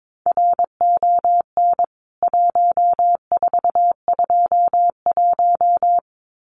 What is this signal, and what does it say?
Morse code, heard as a keyed tone, about 22 wpm, 700 Hz. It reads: RON 1421
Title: CWT Exchange 22wpm